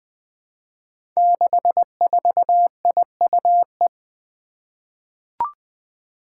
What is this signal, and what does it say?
Morse code, heard as a keyed tone, about 20 wpm, 700 Hz. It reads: 64IUE
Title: Five Challenging Letter-Number Combinations 20wpm